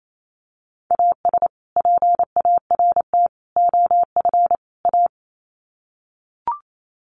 Morse code, heard as a keyed tone, about 28 wpm, 700 Hz. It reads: AS PART OF A
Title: Sets of 4 Words 28wpm